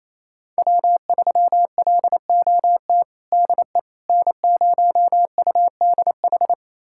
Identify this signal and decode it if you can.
Morse code, heard as a keyed tone, about 28 wpm, 700 Hz. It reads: W3LOT DE N0UDH
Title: USA Callsign DE Callsign - 5 Characters - 28wpm